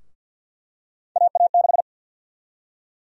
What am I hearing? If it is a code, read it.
Morse code, heard as a keyed tone, about 52 wpm, 700 Hz. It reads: AA6
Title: USA Call Sign Prefixes - 50wpm